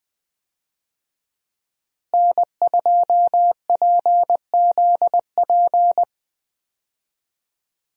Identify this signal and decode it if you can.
Morse code, heard as a keyed tone, about 20 wpm, 700 Hz. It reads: N2PZP
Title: USA Call Signs - 5 Characters - Repeated 2x - 20wpm